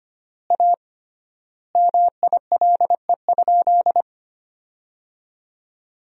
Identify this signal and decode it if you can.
Morse code, heard as a keyed tone, about 25 wpm, 700 Hz. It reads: A MILE?
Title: 3x Word-Spacing - Sentences from Top 400 Words - 25wpm